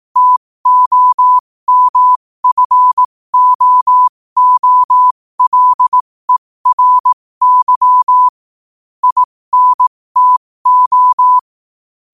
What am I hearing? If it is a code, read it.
Morse code, heard as a keyed tone, about 18 wpm, spaced as for 15 wpm, 1000 Hz. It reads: TOMFOOLERY INTO
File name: qotdfiles / 15wpm